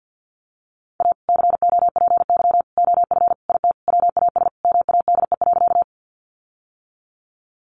Morse code, heard as a keyed tone, about 53 wpm, 700 Hz. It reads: A COPY OF IT WAS MADE.
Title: Sentences from Top 1400 Words 50wpm